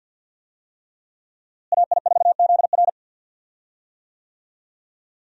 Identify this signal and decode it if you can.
Morse code, heard as a keyed tone, about 52 wpm, 700 Hz. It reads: AI4ZR
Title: USA Call Signs - 5 Characters - Repeated 3x - 50wpm